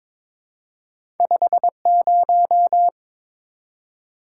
Morse code, heard as a keyed tone, about 22 wpm, 690 Hz. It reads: 50
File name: practice / 22wpm / Two Letter-Number Combinations - 22wpm